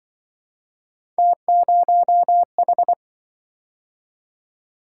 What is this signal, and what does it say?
Morse code, heard as a keyed tone, about 25 wpm, 700 Hz. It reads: T0H
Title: Speed-Racing - Three Letter-Number Combinations - 20wpm